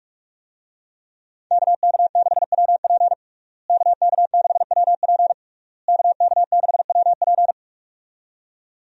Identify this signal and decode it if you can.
Morse code, heard as a keyed tone, about 46 wpm, 700 Hz. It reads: KK6WP KK6WP KK6WP
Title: USA Call Signs - 5 Characters - Repeated 3x - 45wpm